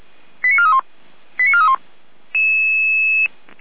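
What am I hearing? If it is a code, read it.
Five-tone selective call (ZVEI), heard as quick runs of five tones, tones of about 70 ms. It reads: 89431 89431
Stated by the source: Received on an SDR, ZVEI five-tone code 89457.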